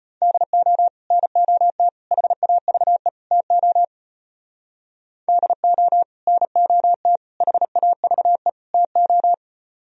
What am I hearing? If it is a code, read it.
Morse code, heard as a keyed tone, about 33 wpm, 700 Hz. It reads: DO NOT HAVE TO DO NOT HAVE TO